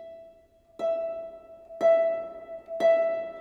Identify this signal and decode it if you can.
There is no code.